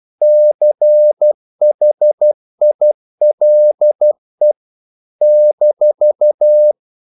Morse code, heard as a keyed tone, about 12 wpm, 600 Hz. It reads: CHILE -
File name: CW / Practice / NationalCapitals12